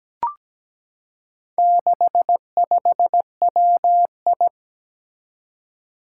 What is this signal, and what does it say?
Morse code, heard as a keyed tone, about 17 wpm, 700 Hz. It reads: 65WI
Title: Four Challenging Letter-Number Combinations 17wpm